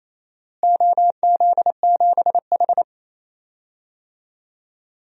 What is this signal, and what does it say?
Morse code, heard as a keyed tone, about 28 wpm, 700 Hz. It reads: OZ7H
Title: DXCC-100 Call Signs - 4 Characters - Repeated 2x - 28wpm